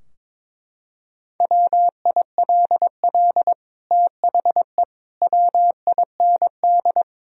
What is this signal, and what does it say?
Morse code, heard as a keyed tone, about 22 wpm, 700 Hz. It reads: WILL THE WIND
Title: Sentences from Top 1000 Words - Review Part 2 - 22wpm